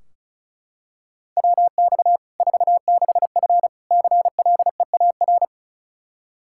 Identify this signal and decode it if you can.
Morse code, heard as a keyed tone, about 34 wpm, 700 Hz. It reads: WX 46F CLEAR
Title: Practice QSO - Exchange 2 - Info - 35wpm